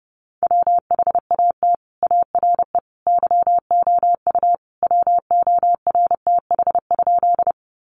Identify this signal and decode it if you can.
Morse code, heard as a keyed tone, about 30 wpm, 700 Hz. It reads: WHAT ARE YOU WORTH?
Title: Sentences from Top 800 Words 30wpm